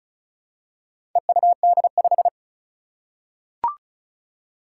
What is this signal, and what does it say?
Morse code, heard as a keyed tone, about 36 wpm, 700 Hz. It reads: EUD5
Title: Four Challenging Letter-Number Combinations 35wpm